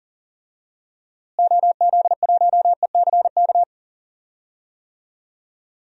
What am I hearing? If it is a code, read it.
Morse code, heard as a keyed tone, about 40 wpm, 700 Hz. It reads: OZ1ECK
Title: DXCC-100 Call Signs - 6 Characters - Repeated 3x - 40wpm